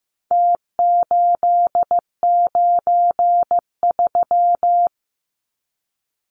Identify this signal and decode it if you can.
Morse code, heard as a keyed tone, about 15 wpm, 700 Hz. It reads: T893
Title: Numbers - Contest Style - 4-Digit 15wpm